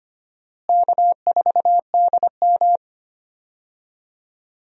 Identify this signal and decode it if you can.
Morse code, heard as a keyed tone, about 25 wpm, 700 Hz. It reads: K4DM
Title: USA Call Signs - 4 Characters - Repeated 2x - 25wpm